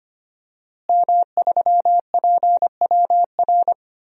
Morse code, heard as a keyed tone, about 25 wpm, 700 Hz. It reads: M3PWR
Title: DXCC-100 Call Signs - 5 Characters - Repeated 2x - 25wpm